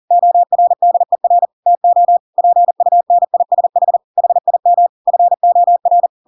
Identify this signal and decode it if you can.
Morse code, heard as a keyed tone, about 40 wpm, 700 Hz. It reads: ORDER TO PUNISH HIM FOR